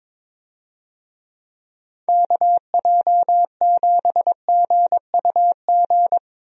Morse code, heard as a keyed tone, about 22 wpm, 700 Hz. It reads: KJ7GUG
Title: USA Call Signs - 6 Characters - 22wpm